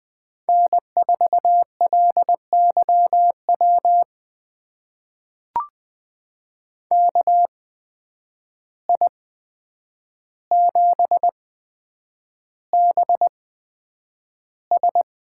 Morse code, heard as a keyed tone, about 20 wpm, spaced as for 9 wpm, 700 Hz. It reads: N4LYW KI7BS
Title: USA Call Signs - 8x Character Spacing - 20wpm